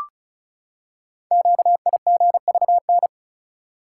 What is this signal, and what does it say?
Morse code, heard as a keyed tone, about 36 wpm, 690 Hz. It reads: QIGVN
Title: Code Groups - 5 Letters - 35wpm